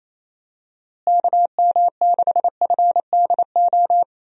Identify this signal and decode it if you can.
Morse code, heard as a keyed tone, about 28 wpm, 700 Hz. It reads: KM6FDO
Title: USA Call Signs - 6 Characters - Repeated 2x - 28wpm